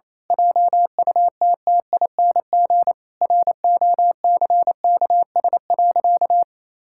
Morse code, heard as a keyed tone, about 27 wpm, 700 Hz. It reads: JUTTING ROCKS.
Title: Pangram Challenge v2 - 28wpm